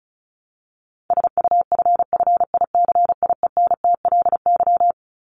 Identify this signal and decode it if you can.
Morse code, heard as a keyed tone, about 34 wpm, 700 Hz. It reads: SUFFICIENTLY
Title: LY Suffix - Top 100 Words with Base Words 35wpm